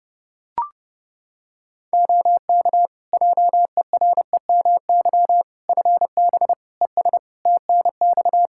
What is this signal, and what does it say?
Morse code, heard as a keyed tone, about 30 wpm, 700 Hz. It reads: OK JEREMY FB ES TNX
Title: Practice QSO - Exchange 2 - Info - 30wpm